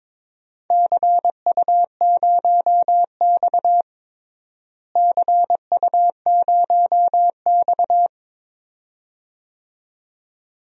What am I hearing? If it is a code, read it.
Morse code, heard as a keyed tone, about 21 wpm, 700 Hz. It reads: CU0X CU0X
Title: DXCC-100 Call Signs - 4 Characters - Repeated 2x - 22wpm